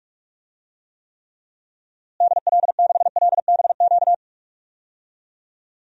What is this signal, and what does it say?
Morse code, heard as a keyed tone, about 46 wpm, 700 Hz. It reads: DL6LBQ
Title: DXCC-100 Call Signs - 6 Characters - Repeated 3x - 45wpm